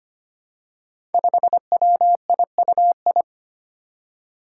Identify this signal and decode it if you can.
Morse code, heard as a keyed tone, about 25 wpm, 700 Hz. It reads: 5WIUI
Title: Five Challenging Letter-Number Combinations 25wpm